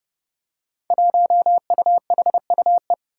Morse code, heard as a keyed tone, about 30 wpm, 700 Hz. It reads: 1UHUE
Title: Five Challenging Letter-Number Combinations 30wpm